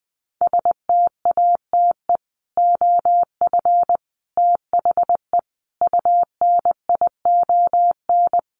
Morse code, heard as a keyed tone, about 20 wpm, 700 Hz. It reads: STATE OF THE UNION
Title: Sets of 5 Words Encore 20wpm